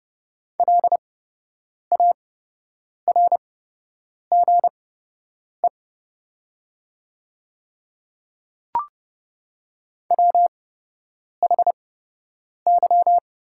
Morse code, heard as a keyed tone, about 30 wpm, spaced as for 8 wpm, 690 Hz. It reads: LARGE WHY
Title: 8x Character Spacing - Top 200 Words - 30wpm